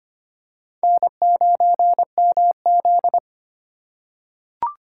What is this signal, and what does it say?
Morse code, heard as a keyed tone, about 25 wpm, 700 Hz. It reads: N9MZ
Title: USA Call Signs - 4x Character Spacing - 25wpm